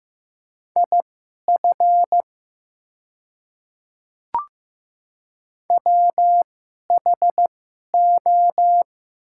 Morse code, heard as a keyed tone, about 15 wpm, spaced as for 11 wpm, 700 Hz. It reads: IF WHO
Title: 2x Character Spacing - Top 200 Words - 15wpm